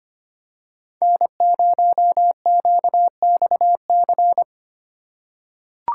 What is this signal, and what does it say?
Morse code, heard as a keyed tone, about 25 wpm, 700 Hz. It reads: N0QXC
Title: USA Call Signs - 6x Character Spacing - 25wpm